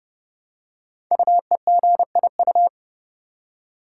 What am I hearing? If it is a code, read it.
Morse code, heard as a keyed tone, about 30 wpm, 700 Hz. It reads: UEGIU
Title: Code Groups - 5 Letters - 30wpm